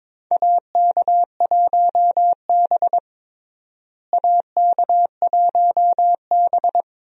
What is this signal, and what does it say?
Morse code, heard as a keyed tone, about 22 wpm, 700 Hz. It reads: AK1B AK1B
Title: USA Call Signs - 4 Characters - Repeated 2x - 22wpm